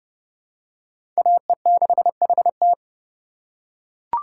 Morse code, heard as a keyed tone, about 31 wpm, 690 Hz. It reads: AE6HT
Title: USA Call Signs - 5 Characters - 6x Character Spacing - 30wpm